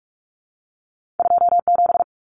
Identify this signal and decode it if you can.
Morse code, heard as a keyed tone, about 46 wpm, 700 Hz. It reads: 27
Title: Common Cut Numbers - 2-Digit 45wpm